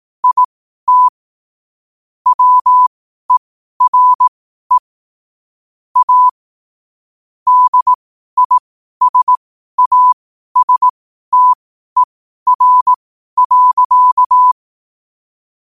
Morse code, heard as a keyed tone, about 18 wpm, spaced as for 12 wpm, 1000 Hz. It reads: IT WERE A DISASTER.
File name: qotdfiles / 12wpm